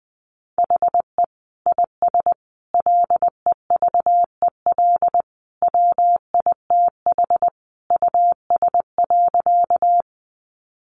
Morse code, heard as a keyed tone, about 20 wpm, 700 Hz. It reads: HE IS LEVEL WITH US.
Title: Sentences from Top 700 Words 20wpm